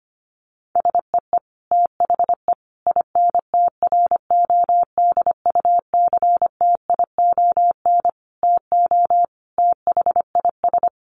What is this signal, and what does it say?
Morse code, heard as a keyed tone, about 25 wpm, 700 Hz. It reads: SEE THE INTRODUCTION TO THIS
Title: Sentences from Top 1300 Words 25wpm